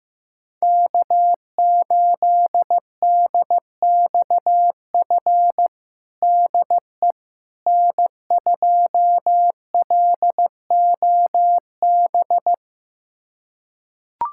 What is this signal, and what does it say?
Morse code, heard as a keyed tone, about 15 wpm, 700 Hz. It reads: K8DXF DE N2LOB
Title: USA Callsign DE Callsign - 5 Characters - 15wpm